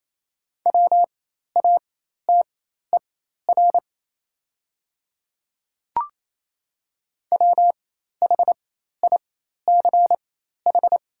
Morse code, heard as a keyed tone, about 28 wpm, spaced as for 13 wpm, 700 Hz. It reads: WATER WHICH
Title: 4x Character Spacing - Top 100 Words - 28wpm